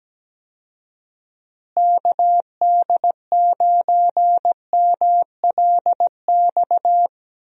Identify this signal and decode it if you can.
Morse code, heard as a keyed tone, about 17 wpm, 700 Hz. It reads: KD9MLX
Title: USA Call Signs - 6 Characters - Repeated 3x - 17wpm